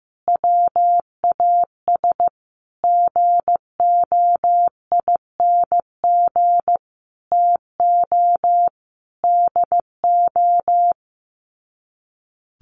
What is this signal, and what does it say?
Morse code, heard as a keyed tone, about 15 wpm, 700 Hz. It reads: WAS GOING TO DO
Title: Sets of 5 Words Encore2 15wpm